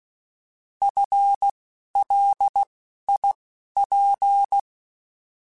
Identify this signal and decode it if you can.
Morse code, heard as a keyed tone, about 16 wpm, spaced as for 11 wpm, 800 Hz. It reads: FLIP
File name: Alphabetical / Alphabetical mp3s / Fs @10